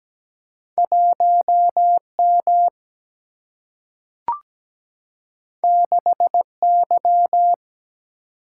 Morse code, heard as a keyed tone, about 17 wpm, 700 Hz. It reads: 1M 6Y
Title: Single-Letter Single-Number 17wpm